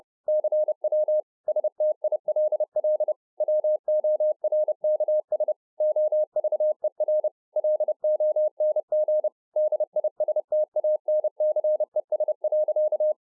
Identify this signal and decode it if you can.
Morse code, heard as a keyed tone, about 30 wpm, 610 Hz. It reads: CW STILL WORKS OVER LONG DISTANCES.